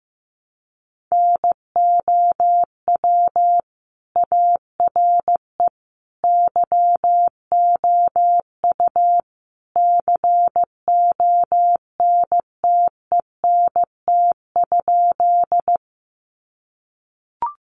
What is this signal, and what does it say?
Morse code, heard as a keyed tone, about 15 wpm, 700 Hz. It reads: NOW ARE YOU CONTENT?
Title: Sentences from Top 900 Words 15wpm